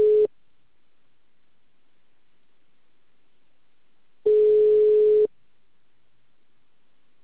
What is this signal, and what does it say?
Ringback tone (Europe, 425 Hz).
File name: out- -2044- - - .37417